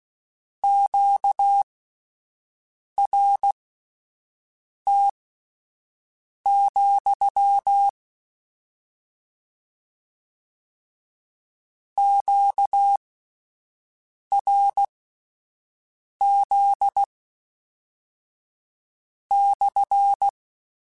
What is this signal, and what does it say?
Morse code, heard as a keyed tone, about 16 wpm, spaced as for 5 wpm, 800 Hz. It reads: QRT, QRZ /